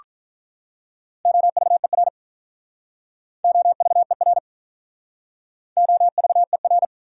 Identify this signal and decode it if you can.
Morse code, heard as a keyed tone, about 45 wpm, 700 Hz. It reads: OVER OVER OVER